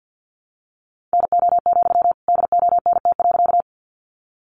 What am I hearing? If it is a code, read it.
Morse code, heard as a keyed tone, about 50 wpm, 700 Hz. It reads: NO, DONT.
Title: Sentences from Top 300 Words 50wpm